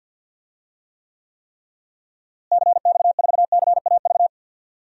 Morse code, heard as a keyed tone, about 52 wpm, 700 Hz. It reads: CX4CAV
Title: DXCC-100 Call Signs - 6 Characters - Repeated 3x - 50wpm